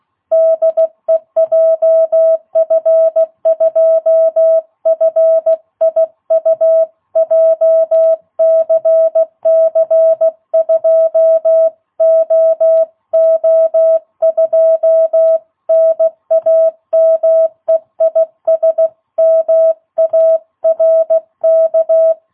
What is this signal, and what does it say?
Morse code, heard as a keyed tone, about 16 wpm, 660 Hz. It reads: DE JF2FIU JCC2OO2 NAME IS MARK
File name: jh8cbh / 2022cw